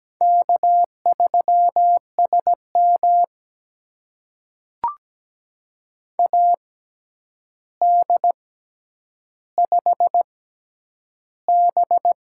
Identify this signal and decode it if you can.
Morse code, heard as a keyed tone, about 17 wpm, spaced as for 9 wpm, 700 Hz. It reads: K3SM AD5B
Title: USA Call Signs - 4 Characters - 6x Character Spacing - 17wpm